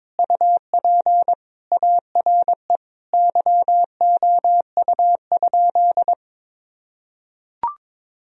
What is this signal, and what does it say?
Morse code, heard as a keyed tone, about 22 wpm, 700 Hz. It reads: UP ARE YOU?
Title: Sentences from Top 1000 Words - Review Part 1 - 22wpm